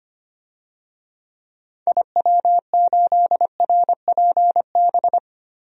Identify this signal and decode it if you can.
Morse code, heard as a keyed tone, about 25 wpm, 700 Hz. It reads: IW8RPB
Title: DXCC-100 Call Signs - 6 Characters - 25wpm